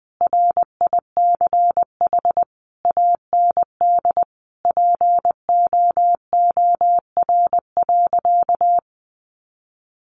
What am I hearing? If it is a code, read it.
Morse code, heard as a keyed tone, about 20 wpm, 700 Hz. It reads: RICH AND POOR.
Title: Sentences from Top 1100 Words 20wpm